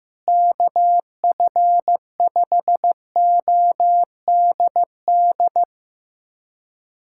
Morse code, heard as a keyed tone, about 16 wpm, 700 Hz. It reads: KF5ODD
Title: USA Callsign DE Callsign 15wpm